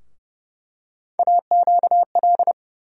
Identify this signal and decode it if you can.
Morse code, heard as a keyed tone, about 30 wpm, 700 Hz. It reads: AQL
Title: USA Call Sign Postfixes - 3 Letters 30wpm